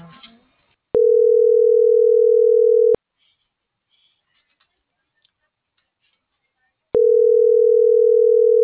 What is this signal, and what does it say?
Ringback tone (North America), 440 + 480 Hz.